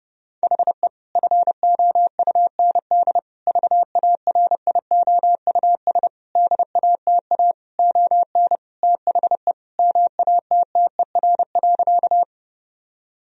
Morse code, heard as a keyed tone, about 32 wpm, 700 Hz. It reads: HE FOUND VARIOUS DATA ON THE MATTER.
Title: Sentences from Top 1000 Words - Review Part 4 - 30wpm